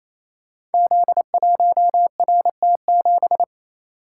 Morse code, heard as a keyed tone, about 28 wpm, 700 Hz. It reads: Z1RT7
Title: Code Groups - 5 Letter:Numbers - 28wpm